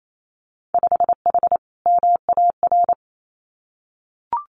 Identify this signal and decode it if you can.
Morse code, heard as a keyed tone, about 28 wpm, 700 Hz. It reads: 5H MAR
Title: Winter Field Day Exchanges Encore 28wpm